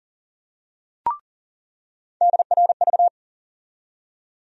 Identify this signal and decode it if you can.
Morse code, heard as a keyed tone, about 40 wpm, 700 Hz. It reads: DRV
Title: USA Call Sign Postfixes - 3 Letters 40wpm